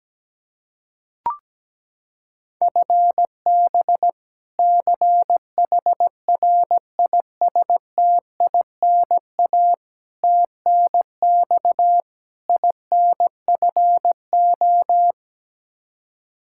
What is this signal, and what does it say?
Morse code, heard as a keyed tone, about 17 wpm, 700 Hz. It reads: FB CHRISTINA TNX INFO
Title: Practice QSO - Exchange 3 - More Info - 17wpm